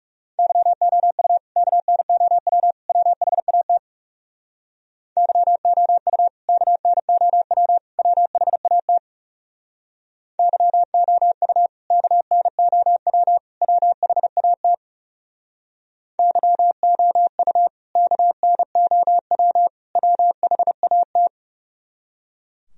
Morse code, heard as a keyed tone, about 36 wpm, 700 Hz. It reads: YOU KNOW WHAT YOU KNOW WHAT YOU KNOW WHAT YOU KNOW WHAT
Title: Speed-Racing - Sets of 3 Words 30wpm